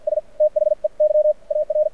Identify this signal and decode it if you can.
Morse code, heard as a keyed tone, about 53 wpm, spaced as for 44 wpm, 590 Hz. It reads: S THEY AR
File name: archived / CRSnet